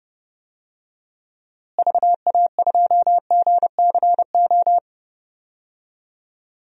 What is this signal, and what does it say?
Morse code, heard as a keyed tone, about 30 wpm, 700 Hz. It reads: VA2GCO
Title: DXCC-100 Call Signs - 6 Characters - Repeated 2x - 30wpm